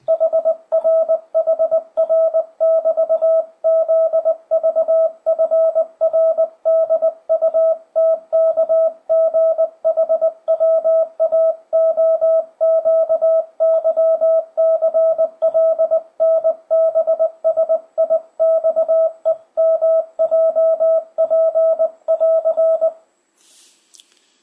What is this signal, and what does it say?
Morse code, heard as a keyed tone, about 19 wpm, 650 Hz. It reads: HR HR<BT>ZVFRDUTKGHWAOQYCLNBSIXEMJP<AR>
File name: 2025cw / speedup / 20wpm